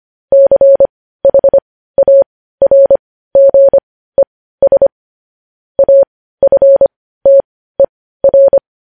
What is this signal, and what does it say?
Morse code, heard as a keyed tone, about 25 wpm, spaced as for 15 wpm, 550 Hz. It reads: CHARGES AFTER
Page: News Headlines in Morse Code at 15 WPM